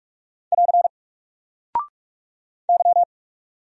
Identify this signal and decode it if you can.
Morse code, heard as a keyed tone, about 46 wpm, 700 Hz. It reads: <AR> Y
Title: Single Character (Letter:Number:Punctuation:Prosign) 45wpm